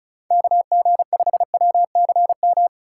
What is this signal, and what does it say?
Morse code, heard as a keyed tone, about 35 wpm, 700 Hz. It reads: KG5WCM
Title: USA Call Signs - 6 Characters - 3x Character Spacing - 35wpm